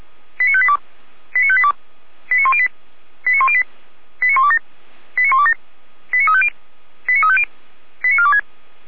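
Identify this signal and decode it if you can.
Five-tone selective call (ZVEI), heard as quick runs of five tones, tones of about 70 ms. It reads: 89582 89582 89108 89108 89127 89127 89350 89350 89426
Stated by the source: Five-tone ZVEI selective call received on an SDR receiver.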